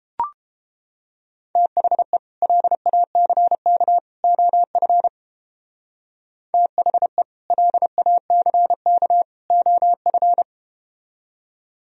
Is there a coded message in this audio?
Yes: Morse code from 1.6 to 10.4 s, heard as a keyed tone, about 32 wpm, 700 Hz. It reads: THE LACK OF THE LACK OF